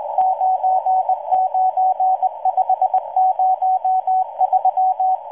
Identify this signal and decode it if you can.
Morse code, heard as a keyed tone, about 19 wpm, 750 Hz, with radio noise. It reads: 99503